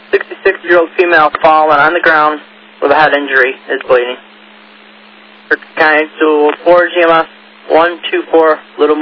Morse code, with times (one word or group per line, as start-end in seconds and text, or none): none